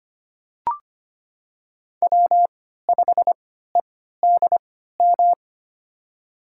Morse code, heard as a keyed tone, about 25 wpm, spaced as for 14 wpm, 700 Hz. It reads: W5EDM
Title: USA Call Signs - 5 Characters - 3x Character Spacing - 25wpm